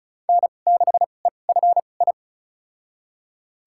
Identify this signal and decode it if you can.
Morse code, heard as a keyed tone, about 34 wpm, spaced as for 25 wpm, 700 Hz. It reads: N6EFI
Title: USA Call Signs - 5 Characters - 2x Character Spacing - 35wpm